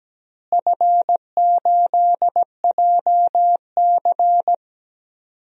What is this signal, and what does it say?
Morse code, heard as a keyed tone, about 17 wpm, 700 Hz. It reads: F8JC
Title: DXCC-100 Call Signs - 4 Characters - Repeated 2x - 17wpm